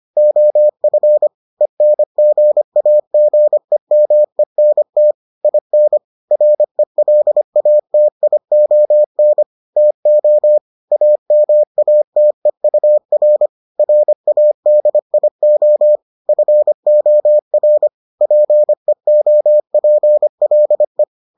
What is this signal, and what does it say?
Morse code, heard as a keyed tone, about 25 wpm, 600 Hz. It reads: OF ENGAGEMENT IN RELATION TO AMATEUR RADIO FOR PEOPLE